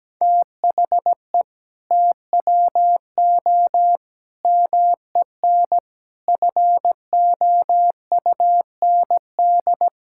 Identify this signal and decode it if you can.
Morse code, heard as a keyed tone, about 17 wpm, 700 Hz. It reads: THE TWO MEN FOUND
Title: Sentences from Top 1000 Words - Review Part 5 - 17wpm